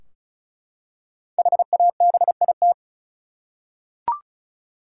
Morse code, heard as a keyed tone, about 36 wpm, 700 Hz. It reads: HABIT
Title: Five Letter Words - 35wpm